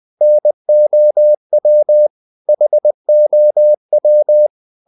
Morse code, heard as a keyed tone, about 20 wpm, 600 Hz. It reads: NOW HOW
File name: CW / Practice / 500MostCommonEnglishWords20